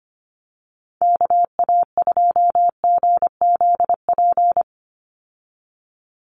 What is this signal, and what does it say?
Morse code, heard as a keyed tone, about 25 wpm, 700 Hz. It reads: KA2GZP
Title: USA Call Signs 25wpm